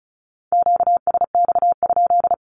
Morse code, heard as a keyed tone, about 34 wpm, 700 Hz. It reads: QSX?
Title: Q-Codes 35wpm